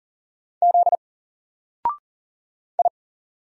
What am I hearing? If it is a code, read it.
Morse code, heard as a keyed tone, about 38 wpm, 700 Hz. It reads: Z I
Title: Single Letter-Number Alt 40wpm